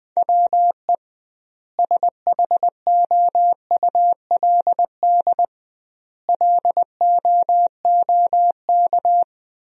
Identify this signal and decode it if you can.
Morse code, heard as a keyed tone, about 20 wpm, 700 Hz. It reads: WE SHOULD LOOK
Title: 2x Word-Spacing - Sentences from Top 300 Words - 20wpm